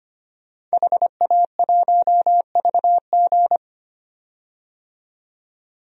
Morse code, heard as a keyed tone, about 25 wpm, 700 Hz. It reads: HA1VG
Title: DXCC-100 Call Signs - 5 Characters - 25wpm